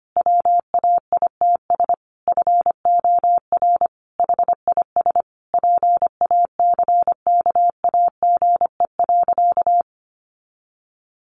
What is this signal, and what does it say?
Morse code, heard as a keyed tone, about 25 wpm, 700 Hz. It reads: WAITS FOR HIS PACKAGE.
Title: Additional Easy Phrases 25wpm